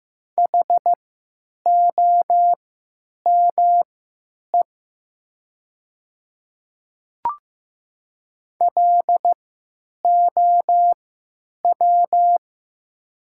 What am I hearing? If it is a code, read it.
Morse code, heard as a keyed tone, about 15 wpm, spaced as for 9 wpm, 700 Hz. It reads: HOME LOW
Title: 3x Character Spacing - Top 300 Words - 15wpm